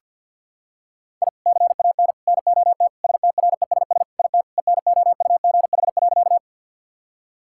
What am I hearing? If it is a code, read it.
Morse code, heard as a keyed tone, about 51 wpm, 700 Hz. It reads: I CAN NOT STRESS IT ENOUGH.
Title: Sentences from Top 1000 Words - Review Part 6 - 50wpm